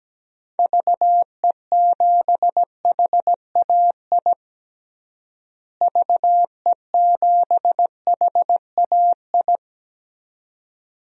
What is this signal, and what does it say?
Morse code, heard as a keyed tone, about 17 wpm, 700 Hz. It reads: VE7HAI VE7HAI
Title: DXCC-100 Call Signs - 6 Characters - Repeated 2x - 17wpm